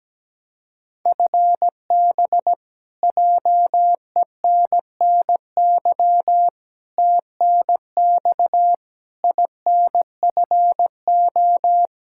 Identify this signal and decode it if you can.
Morse code, heard as a keyed tone, about 17 wpm, 700 Hz. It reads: FB JENNY TNX INFO
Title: K1USN SST QSO Practice 17wpm